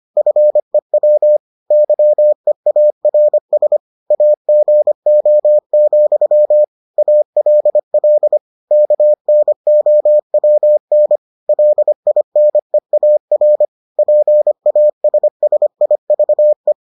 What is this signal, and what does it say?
Morse code, heard as a keyed tone, about 25 wpm, 600 Hz. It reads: FEW YEARS AGO, ALL KNOWN LINEAR PASSIVE